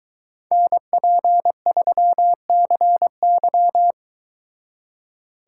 Morse code, heard as a keyed tone, about 22 wpm, 700 Hz. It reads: NP3CY